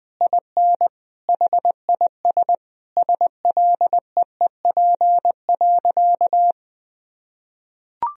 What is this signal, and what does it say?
Morse code, heard as a keyed tone, about 21 wpm, 700 Hz. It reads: IN HIS SLEEP.
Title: Sentences from Top 1000 Words - Review Part 1 - 20wpm